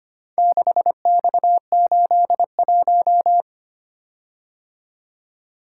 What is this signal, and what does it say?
Morse code, heard as a keyed tone, about 25 wpm, 700 Hz. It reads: 6X81
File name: practice / 25wpm / Four Letter-Number Combinations - 25wpm